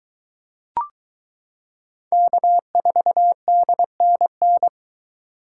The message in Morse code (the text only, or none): K4DNN